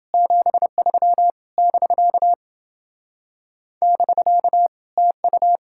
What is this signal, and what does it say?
Morse code, heard as a keyed tone, about 28 wpm, 700 Hz. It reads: 73 <BK> <BK> TU